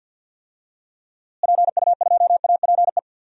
Morse code, heard as a keyed tone, about 52 wpm, 700 Hz. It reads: PU2APE
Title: DXCC-100 Call Signs - 6 Characters - Repeated 3x - 50wpm